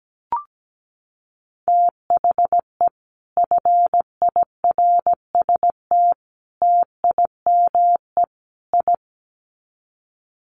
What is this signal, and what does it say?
Morse code, heard as a keyed tone, about 17 wpm, 700 Hz. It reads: THE FIRST TIME I
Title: Sets of 4 Words 17wpm